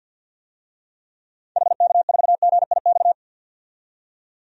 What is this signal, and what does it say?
Morse code, heard as a keyed tone, about 45 wpm, 700 Hz. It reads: HK4GIX
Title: DXCC-100 Call Signs - 6 Characters - 50wpm